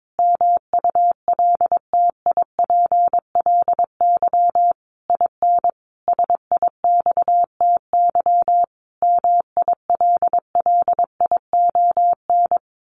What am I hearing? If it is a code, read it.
Morse code, heard as a keyed tone, about 22 wpm, 700 Hz. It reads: MULTIPLY IN SIXTY MILLION
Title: A Sound Of Thunder 22wpm